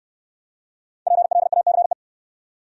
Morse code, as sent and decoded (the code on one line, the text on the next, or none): .--. .-.. .- -.-. .
PLACE